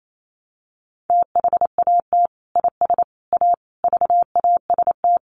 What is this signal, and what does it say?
Morse code, heard as a keyed tone, about 28 wpm, 700 Hz. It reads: THAT IS A VAST
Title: Sentences from Top 1300 Words 28wpm